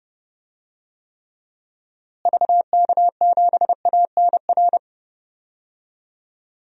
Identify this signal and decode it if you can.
Morse code, heard as a keyed tone, about 31 wpm, 700 Hz. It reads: VK7ANR
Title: DXCC-100 Call Signs - 6 Characters - Repeated 2x - 30wpm